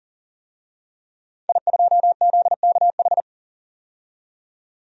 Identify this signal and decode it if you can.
Morse code, heard as a keyed tone, about 37 wpm, 700 Hz. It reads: I2ZKH
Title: DXCC-100 Call Signs - 5 Characters - 40wpm